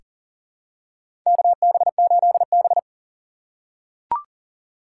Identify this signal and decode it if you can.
Morse code, heard as a keyed tone, about 40 wpm, 700 Hz. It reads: KB8B
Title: USA Call Signs - 4 Characters - 6x Character Spacing - 40wpm